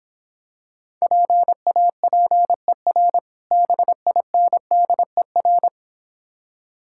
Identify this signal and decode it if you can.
Morse code, heard as a keyed tone, about 26 wpm, 700 Hz. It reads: PAPER BINDER